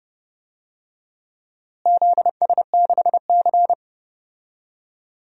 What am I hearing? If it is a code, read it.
Morse code, heard as a keyed tone, about 30 wpm, 700 Hz. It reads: ZS6C
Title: DXCC-100 Call Signs - 4 Characters - Repeated 3x - 30wpm